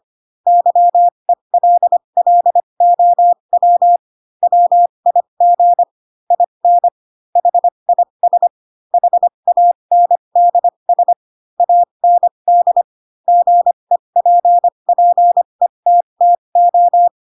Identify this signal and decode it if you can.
Morse code, heard as a keyed tone, about 25 wpm, spaced as for 22 wpm, 700 Hz. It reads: YELLOW WIG IN HIS HANDS AND GEPPETTO
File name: Pinocchio / Pinocchio ch2 18 25 ik1hge